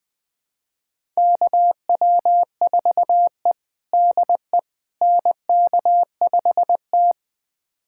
Morse code, heard as a keyed tone, about 20 wpm, 700 Hz. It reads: KW4E DE NK5T
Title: USA Callsign DE Callsign - 4 Characters - 20wpm